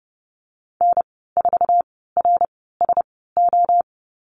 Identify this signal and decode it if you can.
Morse code, heard as a keyed tone, about 30 wpm, spaced as for 17 wpm, 700 Hz. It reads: N4RSO
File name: Lesson 14g - Number 2 - Call Signs 30wpm